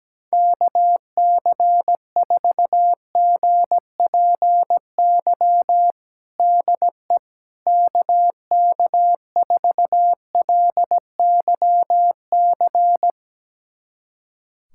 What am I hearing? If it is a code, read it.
Morse code, heard as a keyed tone, about 17 wpm, 700 Hz. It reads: KC4GPY DE KK4LYC
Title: USA Callsign DE Callsign - 6 Characters - 17wpm